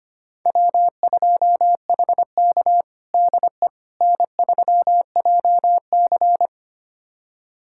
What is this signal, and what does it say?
Morse code, heard as a keyed tone, about 25 wpm, 700 Hz. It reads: W2HK DE N3JC
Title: USA Callsign DE Callsign - 4 Characters - 25wpm